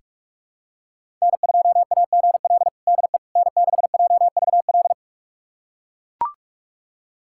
Morse code, heard as a keyed tone, about 41 wpm, 700 Hz. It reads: N2AGL DE N6JVL
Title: USA Callsign DE Callsign - 5 Characters - 45wpm